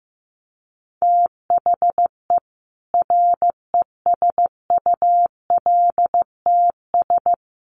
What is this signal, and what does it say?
Morse code, heard as a keyed tone, about 15 wpm, 700 Hz. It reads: THE RESULTS
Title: Sets of 4 Words 15wpm